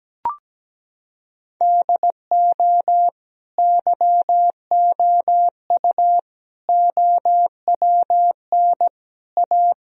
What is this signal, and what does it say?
Morse code, heard as a keyed tone, about 17 wpm, 700 Hz. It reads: DO YOU OWN A
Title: Sentences from Top 1000 Words - Review Part 2 - 17wpm